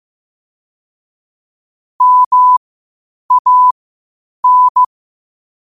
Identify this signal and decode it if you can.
Morse code, heard as a keyed tone, about 15 wpm, spaced as for 8 wpm, 1000 Hz. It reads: MAN
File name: qotdfiles / 8wpm